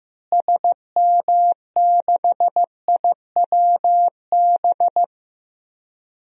Morse code, heard as a keyed tone, about 15 wpm, 700 Hz. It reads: SM6IWB
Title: DXCC-100 Call Signs - 6 Characters - Repeated 2x - 15wpm